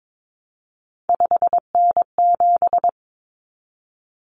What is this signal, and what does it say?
Morse code, heard as a keyed tone, about 22 wpm, 700 Hz. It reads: 5N7
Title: Common Cut Numbers - 3-Digit 22wpm